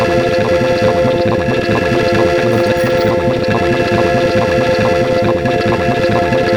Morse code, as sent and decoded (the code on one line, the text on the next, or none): none